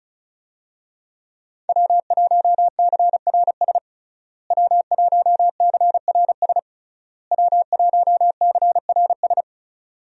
Morse code, heard as a keyed tone, about 36 wpm, 700 Hz. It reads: W1CRS W1CRS W1CRS
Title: USA Call Signs - 5 Characters - Repeated 3x - 35wpm